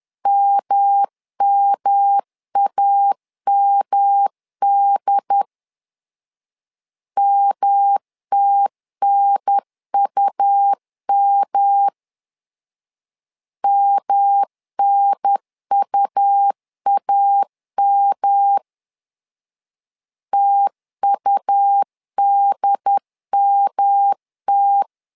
Morse code, heard as a keyed tone, about 10 wpm, 800 Hz. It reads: MMAMD MTNUM MNUAM TUDMT
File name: LEZIONI / 3 MWRO / 3 M 10wpm 1828